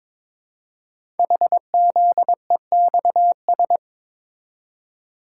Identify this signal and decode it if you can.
Morse code, heard as a keyed tone, about 22 wpm, 700 Hz. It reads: HZEXS
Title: Code Groups - 5 Letters - 22wpm